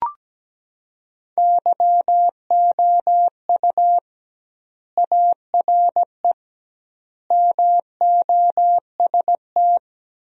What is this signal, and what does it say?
Morse code, heard as a keyed tone, about 17 wpm, 700 Hz. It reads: YOU ARE MOST
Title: 2x Word-Spacing - Sentences from Top 200 Words - 17wpm